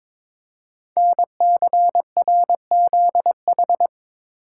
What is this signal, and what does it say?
Morse code, heard as a keyed tone, about 22 wpm, 700 Hz. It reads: NCRZH
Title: Code Groups - 5 Letters - 22wpm